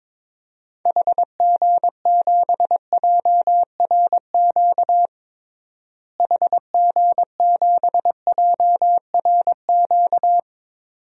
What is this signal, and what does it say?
Morse code, heard as a keyed tone, about 22 wpm, 700 Hz. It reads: HG7JRQ HG7JRQ
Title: DXCC-100 Call Signs - 6 Characters - Repeated 2x - 22wpm